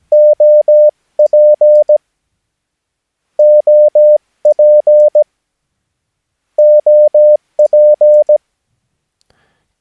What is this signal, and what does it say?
Morse code, heard as a keyed tone, about 17 wpm, 600 Hz. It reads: OP OP OP
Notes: Övningsmaterial med alfabetet A till Ö samt siffror 0 till 9 enligt International/Continental Morse Code.